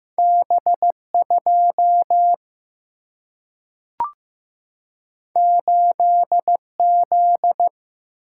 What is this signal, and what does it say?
Morse code, heard as a keyed tone, about 15 wpm, 700 Hz. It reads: B2 8Z
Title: Single-Letter Single-Number 15wpm